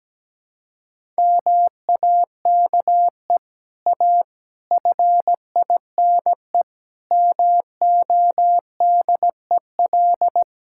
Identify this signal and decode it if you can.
Morse code, heard as a keyed tone, about 17 wpm, 700 Hz. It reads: MAKE A FINE MODEL
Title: Sentences from Top 1000 Words - Review Part 5 - 17wpm